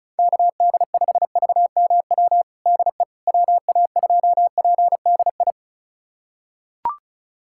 Morse code, heard as a keyed tone, about 36 wpm, 700 Hz. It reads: KD5VMW DE WA2PDI
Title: USA Callsign DE Callsign - 6 Characters - 35wpm